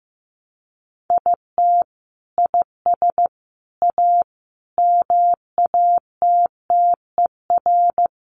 Morse code, heard as a keyed tone, about 15 wpm, 700 Hz. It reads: IT IS A MATTER
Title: Sentences from Top 700 Words 15wpm